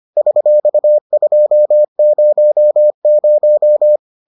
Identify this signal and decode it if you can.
Morse code, heard as a keyed tone, about 25 wpm, 600 Hz. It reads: $200